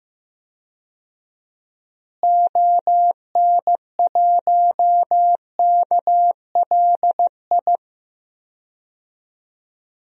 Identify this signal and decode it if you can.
Morse code, heard as a keyed tone, about 15 wpm, 700 Hz. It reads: ON1KLI
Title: DXCC-100 Call Signs - 6 Characters - Repeated 3x - 15wpm